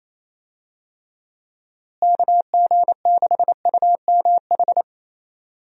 Morse code, heard as a keyed tone, about 28 wpm, 700 Hz. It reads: KG6UMH
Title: USA Call Signs - 6 Characters - Repeated 2x - 28wpm